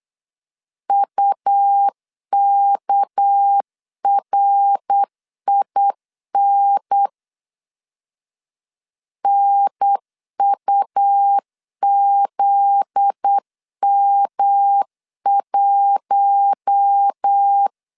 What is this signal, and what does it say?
Morse code, heard as a keyed tone, about 8 wpm, 800 Hz. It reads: UKRIN NUZM1
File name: LEZIONI / 8 YJ45 / 8 YJ4 8wpm 3246